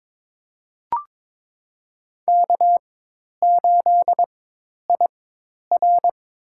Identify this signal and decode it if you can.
Morse code, heard as a keyed tone, about 23 wpm, spaced as for 10 wpm, 700 Hz. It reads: K8IR
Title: USA Call Signs - 4 Characters - 4x Character Spacing - 22wpm